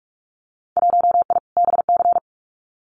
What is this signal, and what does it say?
Morse code, heard as a keyed tone, about 44 wpm, 700 Hz. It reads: 1I BC
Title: Winter Field Day Exchanges Encore 45wpm